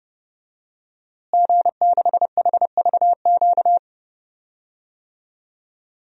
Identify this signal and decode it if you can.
Morse code, heard as a keyed tone, about 30 wpm, 700 Hz. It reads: G6HVQ
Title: DXCC-100 Call Signs - 5 Characters - Repeated 3x - 30wpm